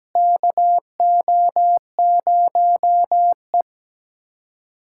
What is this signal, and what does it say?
Morse code, heard as a keyed tone, about 17 wpm, 700 Hz. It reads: KO0E
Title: USA Callsign DE Callsign - 4 Characters - 17wpm